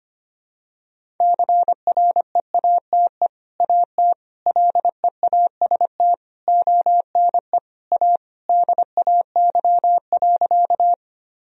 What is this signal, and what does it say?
Morse code, heard as a keyed tone, about 25 wpm, 700 Hz. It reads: CREATE AT LEAST ONE A DAY.
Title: Sentences from Top 1000 Words - Review Part 3 - 25wpm